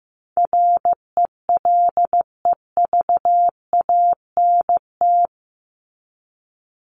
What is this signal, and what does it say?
Morse code, heard as a keyed tone, about 15 wpm, 700 Hz. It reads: RELEVANT
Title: Top 1001-1250 Words 15wpm